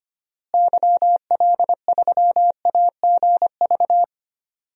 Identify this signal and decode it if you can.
Morse code, heard as a keyed tone, about 25 wpm, 700 Hz. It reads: YL3AGV
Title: DXCC-100 Call Signs - 6 Characters - Repeated 2x - 25wpm